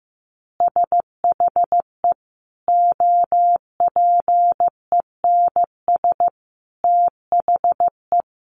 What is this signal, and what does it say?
Morse code, heard as a keyed tone, about 15 wpm, 700 Hz. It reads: SHE OPENS THE
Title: Additional Easy Phrases 15wpm